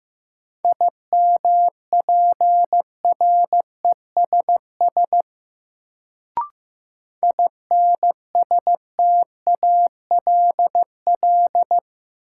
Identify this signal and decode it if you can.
Morse code, heard as a keyed tone, about 15 wpm, 700 Hz. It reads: IMPRESS INSTALL
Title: Sentences from Top 1600 Words - New Word Drill Alt - 15wpm